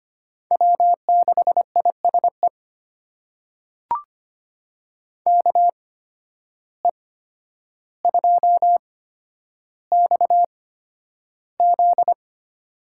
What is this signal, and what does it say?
Morse code, heard as a keyed tone, about 25 wpm, spaced as for 11 wpm, 700 Hz. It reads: W6ISE KE2XZ
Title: USA Call Signs - 5 Characters - 8x Character Spacing - 25wpm